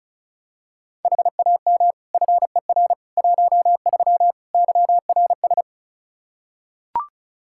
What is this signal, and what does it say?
Morse code, heard as a keyed tone, about 36 wpm, 700 Hz. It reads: HAM FER 13 YRS
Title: Practice QSO - Exchange 3 - More Info - 35wpm